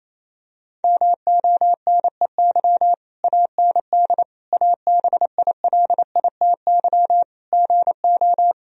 Morse code, heard as a keyed tone, about 27 wpm, 700 Hz. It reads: MONEY AND ABILITY GO
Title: Sentences from Top 1000 Words - Review Part 3 - 28wpm